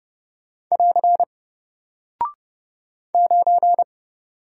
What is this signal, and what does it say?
Morse code, heard as a keyed tone, about 30 wpm, 700 Hz. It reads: <AR> 9
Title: Single Character (Letter:Number:Punctuation:Prosign) Alt 30wpm